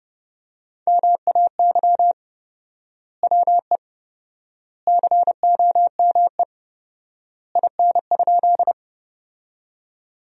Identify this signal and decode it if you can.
Morse code, heard as a keyed tone, about 32 wpm, 700 Hz. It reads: MAY WE COME IN?
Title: 4x Word-Spacing - Sentences from Top 100 Words - 30wpm